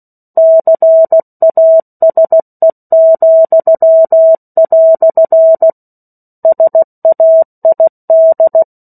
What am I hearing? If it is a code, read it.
Morse code, heard as a keyed tone, about 16 wpm, 650 Hz. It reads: CASE," SAID
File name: 03-Chapter-03 16 WPM